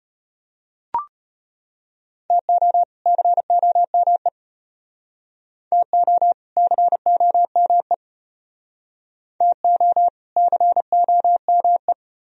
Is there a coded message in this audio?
Yes: Morse code from 2.3 to 11.9 s, heard as a keyed tone, about 34 wpm, 700 Hz. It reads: TO COME TO COME TO COME